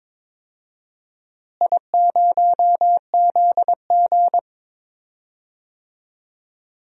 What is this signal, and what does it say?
Morse code, heard as a keyed tone, about 22 wpm, 700 Hz. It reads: I0ZG
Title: DXCC-100 Call Signs - 4 Characters - Repeated 3x - 22wpm